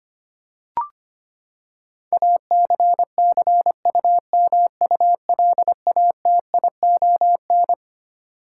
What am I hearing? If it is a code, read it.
Morse code, heard as a keyed tone, about 25 wpm, 700 Hz. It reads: ACCUMULATION
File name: Twelve Letter Words - 25wpm